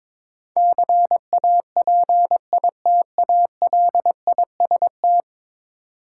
Morse code, heard as a keyed tone, about 21 wpm, 700 Hz. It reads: CAPITALIST
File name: practice / 22wpm / 10 Letter Words - First 3 Letters Given - 22wpm